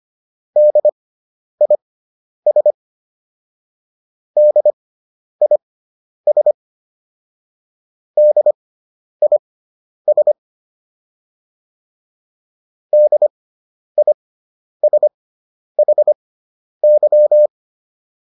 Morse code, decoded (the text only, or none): DIS DIS DIS DISHY